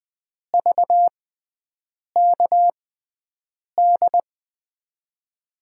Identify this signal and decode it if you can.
Morse code, heard as a keyed tone, about 20 wpm, spaced as for 7 wpm, 700 Hz. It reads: VKD
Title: USA Call Signs - 6 Characters - 6x Character Spacing - 20wpm